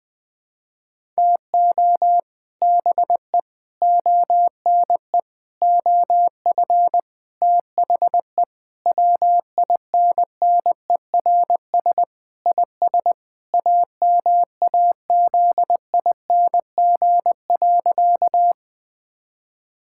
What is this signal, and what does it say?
Morse code, heard as a keyed tone, about 20 wpm, 700 Hz. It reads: TO BE ONE OF THE WINNERS IS AMAZING.
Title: Sentences with 5-Word Phrases - 20wpm